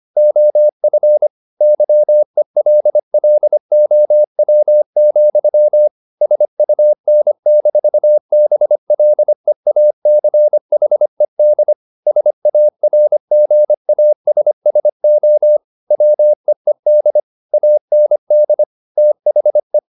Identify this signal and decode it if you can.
Morse code, heard as a keyed tone, about 25 wpm, 600 Hz. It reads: OF YELLOW, SUN-BLEACHED SARGASSO WEED AND THE
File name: the old man and the sea part 1 25wpm